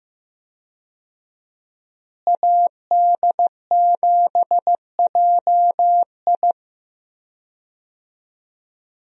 Morse code, heard as a keyed tone, about 15 wpm, 700 Hz. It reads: AD7JI
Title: USA Call Signs - 5 Characters - Repeated 2x - 15wpm